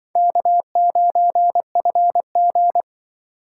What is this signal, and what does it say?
Morse code, heard as a keyed tone, about 25 wpm, 700 Hz. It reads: K9FG
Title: Speed Racing - USA 1x2 Call Signs 20wpm